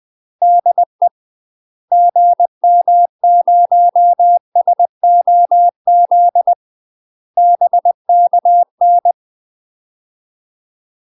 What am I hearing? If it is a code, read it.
Morse code, heard as a keyed tone, about 20 wpm, 700 Hz. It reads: DE GM0SOZ BKN
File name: UK MASTER QSOs TRAINING/20WPM